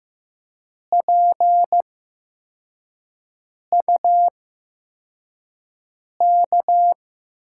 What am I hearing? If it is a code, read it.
Morse code, heard as a keyed tone, about 15 wpm, spaced as for 4 wpm, 700 Hz. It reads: PUK
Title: USA Call Signs - 6 Characters - 8x Character Spacing - 15wpm